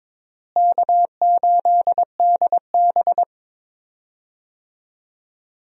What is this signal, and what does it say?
Morse code, heard as a keyed tone, about 22 wpm, 700 Hz. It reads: K8DB
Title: USA Call Signs - 4 Characters - Repeated 2x - 22wpm